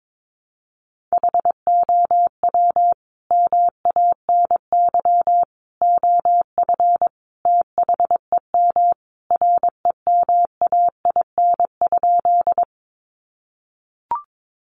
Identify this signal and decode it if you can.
Morse code, heard as a keyed tone, about 22 wpm, 700 Hz. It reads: HOW MANY OF THEM REMAIN?
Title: Sentences from Top 1200 Words 22wpm